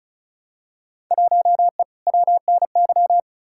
Morse code, heard as a keyed tone, about 34 wpm, 700 Hz. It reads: 1E WNY
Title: Field Day Exchanges - 35wpm